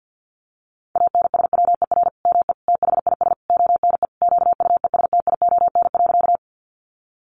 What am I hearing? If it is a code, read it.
Morse code, heard as a keyed tone, about 50 wpm, 700 Hz. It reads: ANSWER ME THIS ONE QUESTION.
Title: Sentences from Top 400 Words 50wpm